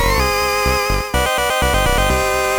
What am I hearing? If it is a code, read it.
There is no code.